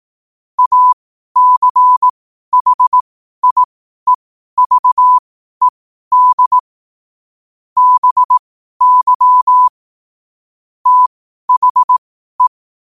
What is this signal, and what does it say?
Morse code, heard as a keyed tone, about 18 wpm, spaced as for 12 wpm, 1000 Hz. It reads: ACHIEVED BY THE
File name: qotdfiles / 12wpm